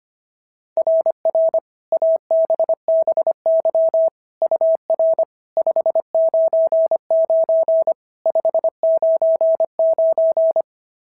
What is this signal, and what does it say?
Morse code, heard as a keyed tone, about 22 wpm, 650 Hz. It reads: RR ABBY UR 599 599
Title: SOTA QSO Practice 25wpm